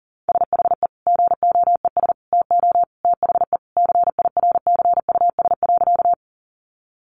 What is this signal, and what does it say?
Morse code, heard as a keyed tone, about 40 wpm, 700 Hz. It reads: SHE GOES TO THE CIRCUS.
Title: Easy Phrases 40wpm